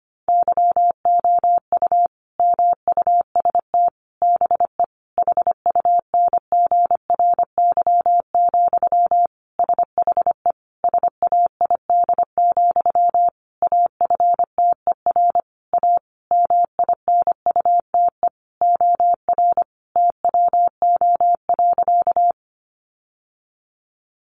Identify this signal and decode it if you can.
Morse code, heard as a keyed tone, about 25 wpm, 700 Hz. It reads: YOU MUST BE HUNGRY, SHE SAID, AFTER A MINUTE OR TWO.
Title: Sentences from Top 1400 Words 25wpm